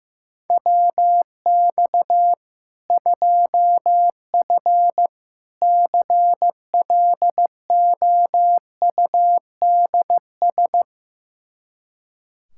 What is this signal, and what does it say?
Morse code, heard as a keyed tone, about 15 wpm, 700 Hz. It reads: WX 2F CLOUDS
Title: Practice QSO - Exchange 2 - Info - 15wpm